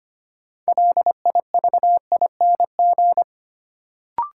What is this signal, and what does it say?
Morse code, heard as a keyed tone, about 25 wpm, 700 Hz. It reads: LIVING
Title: Sets of 4 Words - New Word Drill - 25wpm